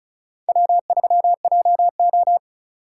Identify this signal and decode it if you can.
Morse code, heard as a keyed tone, about 34 wpm, 700 Hz. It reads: W3JO
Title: USA Call Signs - 3x Character Spacing - 35wpm